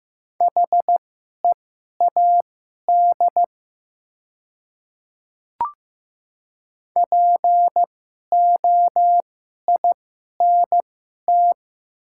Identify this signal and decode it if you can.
Morse code, heard as a keyed tone, about 15 wpm, spaced as for 11 wpm, 700 Hz. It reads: HEAD POINT
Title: 2x Character Spacing - Top 300 Words - 15wpm